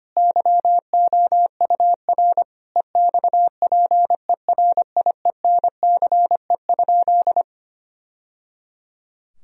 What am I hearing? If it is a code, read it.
Morse code, heard as a keyed tone, about 25 wpm, 700 Hz. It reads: YOUR EXPERIENCE?
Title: Sentences from Top 1000 Words - Review Part 4 - 25wpm